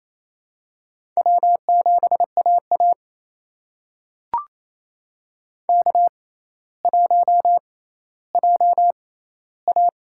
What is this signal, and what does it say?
Morse code, heard as a keyed tone, about 28 wpm, spaced as for 14 wpm, 700 Hz. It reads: W7AA K1JA
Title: USA Call Signs - 4 Characters - 6x Character Spacing - 28wpm